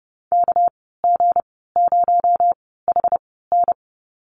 Morse code, heard as a keyed tone, about 30 wpm, spaced as for 17 wpm, 700 Hz. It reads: KG0HN
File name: Lesson 37g - Letter x - Call Signs 30wpm